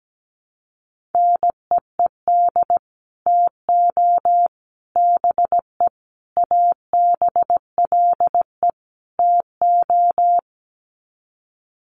Morse code, heard as a keyed tone, about 17 wpm, 700 Hz. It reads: NEED TO BE ABLE TO
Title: Sets of 5 Words Encore 17wpm